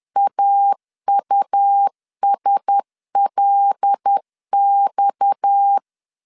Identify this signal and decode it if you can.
Morse code, heard as a keyed tone, about 10 wpm, 800 Hz. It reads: AUSLX
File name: LEZIONI / 6 PKX0 / 6 PKX 10wpm 2426